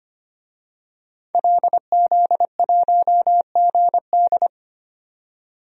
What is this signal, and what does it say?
Morse code, heard as a keyed tone, about 25 wpm, 700 Hz. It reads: LZ1GD
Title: DXCC-100 Call Signs - 5 Characters - 25wpm